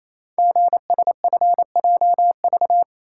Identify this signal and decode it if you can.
Morse code, heard as a keyed tone, about 27 wpm, 700 Hz. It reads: GSFJV
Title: Code Groups - 5 Letters - 28wpm